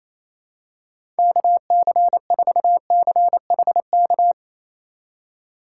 Morse code, heard as a keyed tone, about 28 wpm, 700 Hz. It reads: KC4CHK
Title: USA Call Signs - 2x Character Spacing - 28wpm